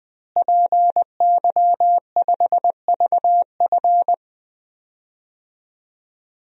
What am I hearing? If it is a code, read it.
Morse code, heard as a keyed tone, about 21 wpm, 700 Hz. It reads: PY5VF
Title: DXCC-100 Call Signs - 5 Characters - Repeated 3x - 20wpm